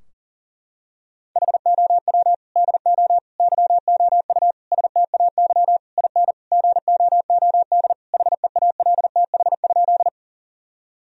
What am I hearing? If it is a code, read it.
Morse code, heard as a keyed tone, about 40 wpm, 700 Hz. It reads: HOW DO YOU STAY IN GOOD HEALTH?
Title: Sentences from Top 1000 Words - Review Part 3 - 40wpm